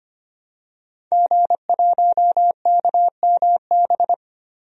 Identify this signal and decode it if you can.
Morse code, heard as a keyed tone, about 25 wpm, 700 Hz. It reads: G1KMB
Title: DXCC-100 Call Signs - 5 Characters - Repeated 2x - 25wpm